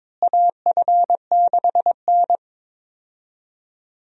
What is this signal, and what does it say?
Morse code, heard as a keyed tone, about 23 wpm, 700 Hz. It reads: AF6N
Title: USA Call Signs - 4 Characters - Repeated 2x - 22wpm